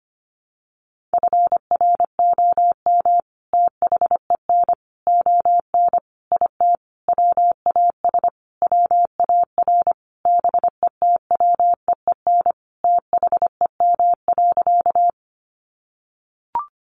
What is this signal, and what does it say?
Morse code, heard as a keyed tone, about 25 wpm, 700 Hz. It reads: FROM THEN ON IT WAS WAR BETWEEN THEM.
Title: Sentences from Top 400 Words 25wpm